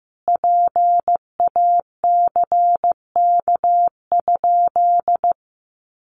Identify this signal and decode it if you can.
Morse code, heard as a keyed tone, about 14 wpm, 700 Hz. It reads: PACK?
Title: Sentences from Top 1300 Words 15wpm